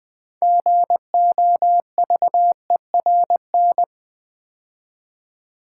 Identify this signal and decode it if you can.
Morse code, heard as a keyed tone, about 20 wpm, 700 Hz. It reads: GOVERN
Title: Build A Word - 2 Letters At A Time - 6-Letter Words - 20wpm